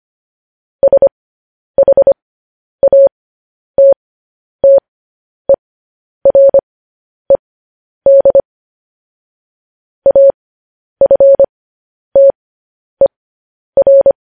Morse code, heard as a keyed tone, about 25 wpm, spaced as for 10 wpm, 550 Hz. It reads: SHATTERED AFTER